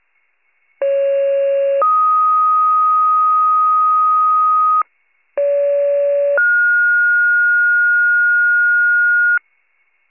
Two-tone page, heard as a one-second tone, then a three-second tone, about 550 then 1250 Hz, about 570 then 1430 Hz, with noise. It is heard from 0.8 to 9.4 s.
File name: pager / 29 FIRE / wcalert